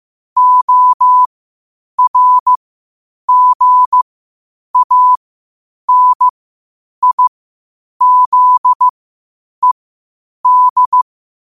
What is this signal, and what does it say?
Morse code, heard as a keyed tone, about 15 wpm, spaced as for 8 wpm, 1000 Hz. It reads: ORGANIZED
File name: qotdfiles / 8wpm